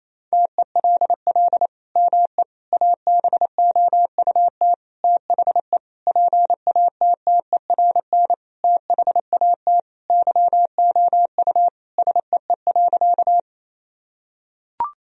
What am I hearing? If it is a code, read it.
Morse code, heard as a keyed tone, about 28 wpm, 700 Hz. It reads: TELL ME ABOUT THE PATTERN THAT YOU SEE.
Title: Sentences from Top 1000 Words - Review Part 2 - 28wpm